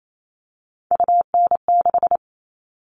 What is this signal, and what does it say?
Morse code, heard as a keyed tone, about 28 wpm, 700 Hz. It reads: UN6
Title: All Cut Numbers - 3-Digit 28wpm